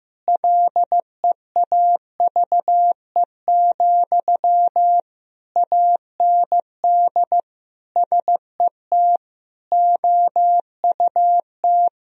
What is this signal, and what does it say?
Morse code, heard as a keyed tone, about 16 wpm, 700 Hz. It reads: LEAVE, AND SET OUT
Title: Sentences from Top 1000 Words - Review Part 1 - 15wpm